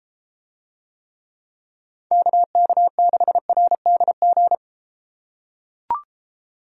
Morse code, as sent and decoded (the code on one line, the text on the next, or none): -.- -.- -.... .-. -.. --.
KK6RDG